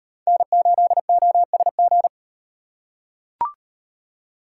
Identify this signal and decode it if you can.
Morse code, heard as a keyed tone, about 35 wpm, 700 Hz. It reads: N8OSG